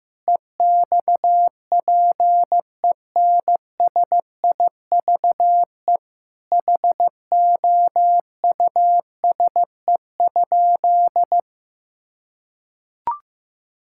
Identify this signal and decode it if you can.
Morse code, heard as a keyed tone, about 15 wpm, 700 Hz. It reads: EXPENSIVE HOUSE?
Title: Sentences from Top 1000 Words - Review Part 6 - 15wpm